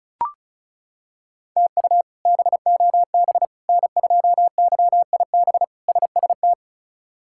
Morse code, heard as a keyed tone, about 34 wpm, 700 Hz. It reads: TU BOB N2YIB SST
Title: K1USN SST QSO Practice - 35wpm